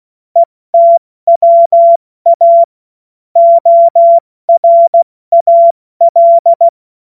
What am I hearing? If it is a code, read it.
Morse code, heard as a keyed tone, about 16 wpm, 680 Hz. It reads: ETWA ORAL
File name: Morsekurs / Witze / Witz 11. Juli 2023 70 F80